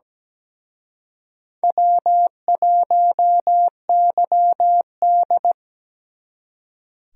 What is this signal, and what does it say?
Morse code, heard as a keyed tone, about 20 wpm, 700 Hz. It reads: W1YD
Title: Speed Racing - USA 1x2 Call Signs 17wpm